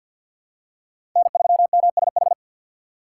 Morse code, heard as a keyed tone, about 47 wpm, 700 Hz. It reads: N3MSH
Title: USA Call Signs - 5 Characters - 4x Character Spacing - 50wpm